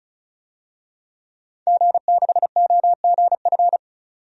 Morse code, heard as a keyed tone, about 36 wpm, 700 Hz. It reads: G6OGF
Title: DXCC-100 Call Signs - 5 Characters - 35wpm